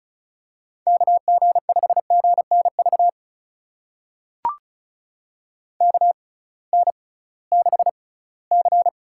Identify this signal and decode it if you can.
Morse code, heard as a keyed tone, about 36 wpm, spaced as for 21 wpm, 700 Hz. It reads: KG5GNV KN6C
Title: USA Call Signs - 6 Characters - 6x Character Spacing - 35wpm